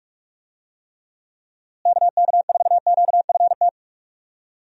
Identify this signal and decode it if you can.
Morse code, heard as a keyed tone, about 45 wpm, 700 Hz. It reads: KK4QFT
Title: USA Call Signs - 6 Characters - Repeated 3x - 45wpm